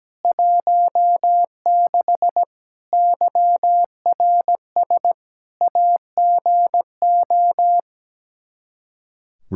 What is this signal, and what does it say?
Morse code, heard as a keyed tone, about 17 wpm, 700 Hz. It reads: 16 YRS AGO
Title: Practice QSO - Exchange 3 - More Info - 17wpm